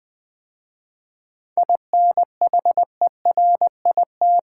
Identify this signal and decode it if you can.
Morse code, heard as a keyed tone, about 20 wpm, 700 Hz. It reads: INHERIT
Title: 7 Letter Words - First 3 Letters Given - 20wpm